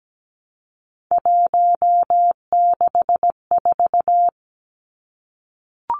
Morse code, heard as a keyed tone, about 17 wpm, 700 Hz. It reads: 164
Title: Numbers - 3-Digit 17wpm